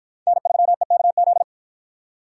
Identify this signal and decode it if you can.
Morse code, heard as a keyed tone, about 53 wpm, 700 Hz. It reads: N3EKZ